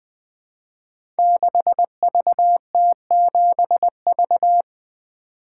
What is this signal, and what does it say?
Morse code, heard as a keyed tone, about 20 wpm, 700 Hz. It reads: 6VT7V
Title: Code Groups - 5 Letter:Numbers - 20wpm